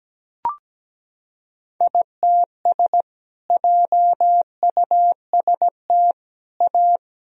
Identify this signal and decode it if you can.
Morse code, heard as a keyed tone, about 17 wpm, 700 Hz. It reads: ITS JUST A
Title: Sentences from Top 1000 Words - Review Part 4 - 17wpm